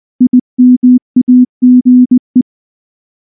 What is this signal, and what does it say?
Morse code, heard as a keyed tone, about 19 wpm, 250 Hz. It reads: IMAGE
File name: morse / morse-training-20WPM-10